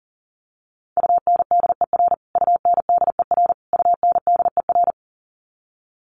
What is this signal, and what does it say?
Morse code, heard as a keyed tone, about 40 wpm, 700 Hz. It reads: UNDER UNDER UNDER
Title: Top 300 Words Repeated 40wpm